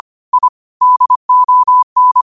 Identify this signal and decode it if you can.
Morse code, heard as a keyed tone, about 25 wpm, 1000 Hz. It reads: I DON